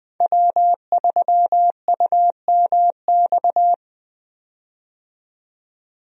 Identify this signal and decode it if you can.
Morse code, heard as a keyed tone, about 20 wpm, 700 Hz. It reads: W3UMX
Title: USA Call Signs - 5 Characters - 20wpm